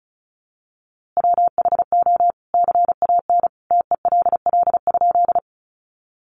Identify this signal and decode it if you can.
Morse code, heard as a keyed tone, about 34 wpm, 700 Hz. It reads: WHO CAN TELL?
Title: A Sound Of Thunder 35wpm